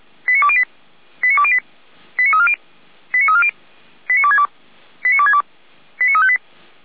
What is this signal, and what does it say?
Five-tone selective call (ZVEI), heard as quick runs of five tones, tones of about 70 ms. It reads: 89208 89208 89340 89340 89262 89262 89358
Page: ZVEI